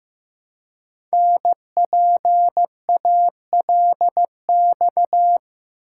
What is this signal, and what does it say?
Morse code, heard as a keyed tone, about 15 wpm, 700 Hz. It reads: NPALX
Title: Code Groups - 5 Letter:Numbers - 15wpm